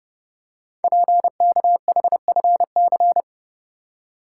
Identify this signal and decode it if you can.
Morse code, heard as a keyed tone, about 30 wpm, 700 Hz. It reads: PKHFC
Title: Code Groups - 5 Letters - 30wpm